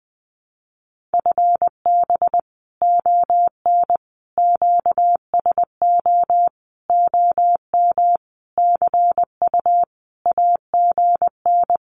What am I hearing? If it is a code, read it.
Morse code, heard as a keyed tone, about 20 wpm, 700 Hz. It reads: FB ON QSO OM CU AGN
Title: Common QSO Phrases 20wpm